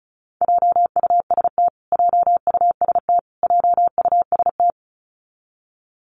Morse code, heard as a keyed tone, about 34 wpm, 700 Hz. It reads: JUST JUST JUST
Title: Top 500 Words Repeated 35wpm